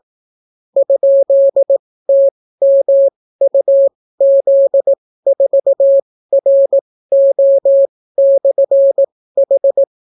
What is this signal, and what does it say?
Morse code, heard as a keyed tone, about 18 wpm, spaced as for 14 wpm, 550 Hz. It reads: ?TMUZ4RO/H